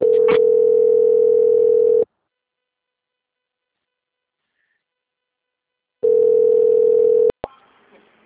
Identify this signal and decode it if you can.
Ringback tone (North America), 440 + 480 Hz.
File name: out- - - - - .27984